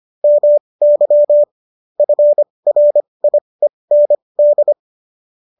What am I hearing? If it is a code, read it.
Morse code, heard as a keyed tone, about 25 wpm, spaced as for 20 wpm, 590 Hz. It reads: MY FRIEND